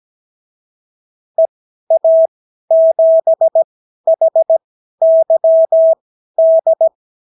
Morse code, heard as a keyed tone, about 17 wpm, spaced as for 12 wpm, 650 Hz. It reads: EA7HYD
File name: CW QSO libroCW4 18-8ppm